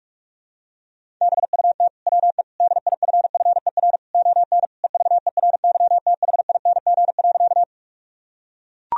Morse code, heard as a keyed tone, about 41 wpm, 700 Hz. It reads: BUT WE DIFFER ON EVERYTHING.
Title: Sentences from Top 1000 Words - Review Part 1 - 45wpm